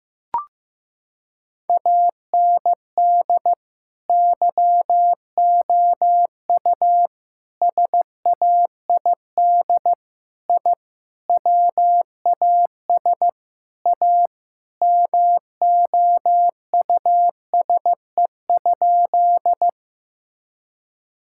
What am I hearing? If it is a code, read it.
Morse code, heard as a keyed tone, about 15 wpm, 700 Hz. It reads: AND YOU SAID I WAS A MOUSE?
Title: Sentences from Top 1000 Words - Review Part 5 - 15wpm